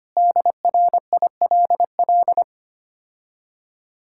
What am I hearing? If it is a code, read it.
Morse code, heard as a keyed tone, about 25 wpm, 700 Hz. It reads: DRILL
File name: practice / 25wpm / Difficult Words - Speed-Racing with Hints - 25wpm